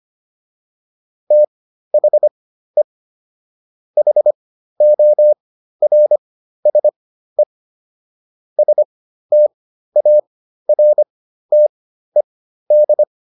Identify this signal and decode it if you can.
Morse code, heard as a keyed tone, about 25 wpm, spaced as for 13 wpm, 600 Hz. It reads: THE HORSE STARTED